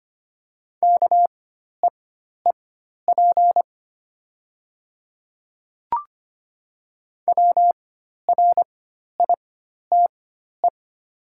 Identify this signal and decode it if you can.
Morse code, heard as a keyed tone, about 25 wpm, spaced as for 12 wpm, 690 Hz. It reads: KEEP WRITE
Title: 4x Character Spacing - Top 500 Words - 25wpm